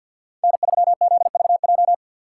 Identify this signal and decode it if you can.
Morse code, heard as a keyed tone, about 61 wpm, 700 Hz. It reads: N3ZVJ
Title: Speed Racing - USA 1x3 Call Signs 50wpm